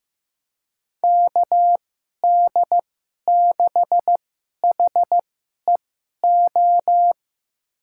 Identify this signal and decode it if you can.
Morse code, heard as a keyed tone, about 15 wpm, spaced as for 11 wpm, 700 Hz. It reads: KD6HEO
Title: USA Call Signs - 6 Characters - 2x Character Spacing - 15wpm